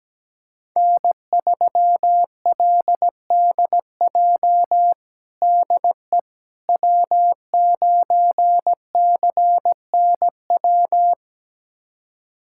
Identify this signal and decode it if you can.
Morse code, heard as a keyed tone, about 17 wpm, 700 Hz. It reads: N3LDJ DE W9CNW
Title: USA Callsign DE Callsign - 5 Characters - 17wpm